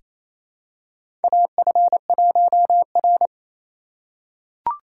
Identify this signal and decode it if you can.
Morse code, heard as a keyed tone, about 28 wpm, 700 Hz. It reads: AF1R
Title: USA Call Signs - 4 Characters - 2x Character Spacing - 28wpm